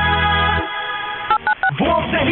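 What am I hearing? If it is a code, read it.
DTMF, heard as telephone touch tones, tones of about 80 ms. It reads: 896
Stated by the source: From a collection of Brazilian radio stations.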